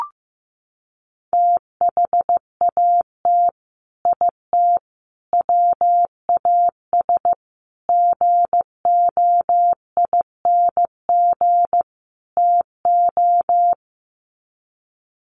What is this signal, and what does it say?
Morse code, heard as a keyed tone, about 15 wpm, 700 Hz. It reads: THAT IT WAS GOING TO
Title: Sets of 5 Words Encore2 15wpm